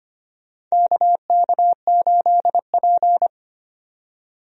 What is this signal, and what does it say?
Morse code, heard as a keyed tone, about 25 wpm, 700 Hz. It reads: KK8P
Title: USA Call Signs - 4 Characters - 25wpm